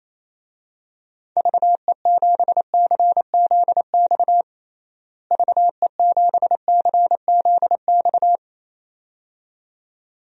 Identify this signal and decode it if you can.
Morse code, heard as a keyed tone, about 28 wpm, 700 Hz. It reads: VE7CZX VE7CZX
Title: DXCC-100 Call Signs - 6 Characters - Repeated 2x - 28wpm